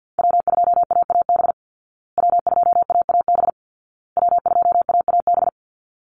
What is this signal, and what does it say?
Morse code, heard as a keyed tone, about 50 wpm, 700 Hz. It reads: W2AAB W2AAB W2AAB
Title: Call Signs Repeated 2x - 50wpm